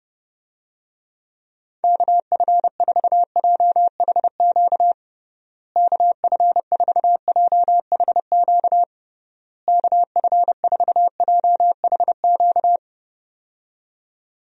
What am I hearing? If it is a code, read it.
Morse code, heard as a keyed tone, about 30 wpm, 700 Hz. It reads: KF4JHQ KF4JHQ KF4JHQ
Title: USA Call Signs - 6 Characters - Repeated 3x - 30wpm